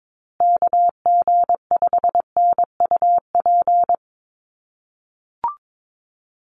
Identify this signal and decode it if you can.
Morse code, heard as a keyed tone, about 22 wpm, 700 Hz. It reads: KG5NUP
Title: USA Call Signs Repeated 2x 22wpm